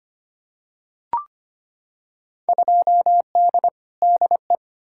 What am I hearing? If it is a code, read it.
Morse code, heard as a keyed tone, about 25 wpm, 700 Hz. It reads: 2D DE
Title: Field Day Exchanges Encore - 25wpm